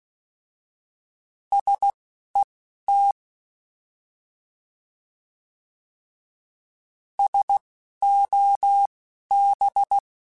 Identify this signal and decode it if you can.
Morse code, heard as a keyed tone, about 16 wpm, spaced as for 11 wpm, 800 Hz. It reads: SET SOB
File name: Alphabetical / Alphabetical mp3s / Ss @10